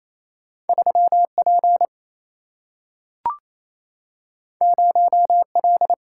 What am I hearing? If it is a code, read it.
Morse code, heard as a keyed tone, about 28 wpm, 700 Hz. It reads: 3P 0L
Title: Single-Letter Single-Number 28wpm